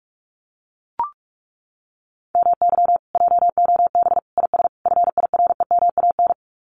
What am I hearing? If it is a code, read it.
Morse code, heard as a keyed tone, about 44 wpm, 700 Hz. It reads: MY JOB IS FIREMAN
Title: Common QSO Phrases Encore 45wpm